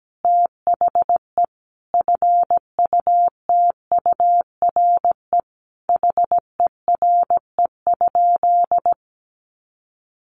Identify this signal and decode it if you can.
Morse code, heard as a keyed tone, about 17 wpm, 700 Hz. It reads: THE FUTURE HERE?
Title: Sentences from Top 1500 Words 17wpm